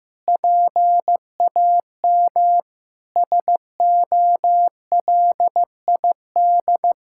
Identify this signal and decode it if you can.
Morse code, heard as a keyed tone, about 15 wpm, 700 Hz. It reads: PAM SOLID
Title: K1USN SST QSO Practice 15wpm